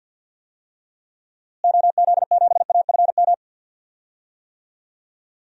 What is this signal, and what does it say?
Morse code, heard as a keyed tone, about 52 wpm, 700 Hz. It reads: OZ7AFM
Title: DXCC-100 Call Signs - 6 Characters - Repeated 3x - 50wpm